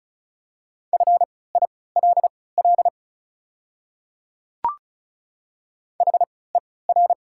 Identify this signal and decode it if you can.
Morse code, heard as a keyed tone, about 34 wpm, spaced as for 20 wpm, 700 Hz. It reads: FILL HER
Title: 3x Character Spacing - Top 500 Words - 35wpm